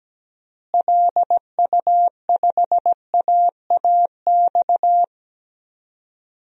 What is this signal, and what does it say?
Morse code, heard as a keyed tone, about 17 wpm, 700 Hz. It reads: LU5AAX
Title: DXCC-100 Call Signs - 6 Characters - Repeated 3x - 17wpm